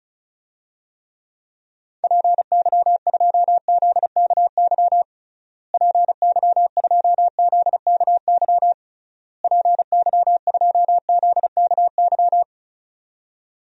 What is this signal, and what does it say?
Morse code, heard as a keyed tone, about 36 wpm, 700 Hz. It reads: PY2ZKY PY2ZKY PY2ZKY
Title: DXCC-100 Call Signs - 6 Characters - Repeated 3x - 35wpm